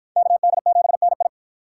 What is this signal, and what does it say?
Morse code, heard as a keyed tone, about 49 wpm, 700 Hz. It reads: KD7NI